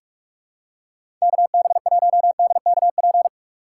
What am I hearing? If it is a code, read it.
Morse code, heard as a keyed tone, about 45 wpm, 700 Hz. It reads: KB1DKP
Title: USA Call Signs - 6 Characters - 2x Character Spacing - 45wpm